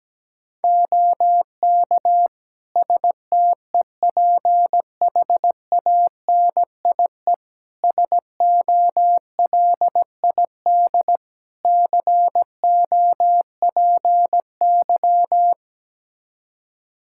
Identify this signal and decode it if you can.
Morse code, heard as a keyed tone, about 17 wpm, 700 Hz. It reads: OK STEPHANIE SOLID COPY
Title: Practice QSO - Exchange 3 - More Info - 17wpm